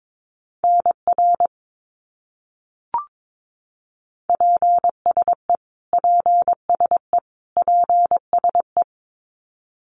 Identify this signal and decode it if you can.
Morse code, heard as a keyed tone, about 22 wpm, 700 Hz. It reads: NR PSE PSE PSE
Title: Abbreviations Repeated 3x - 22wpm